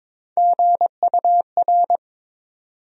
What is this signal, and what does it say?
Morse code, heard as a keyed tone, about 22 wpm, 700 Hz. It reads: GUR
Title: USA Call Sign Postfixes - 3 Letters 22wpm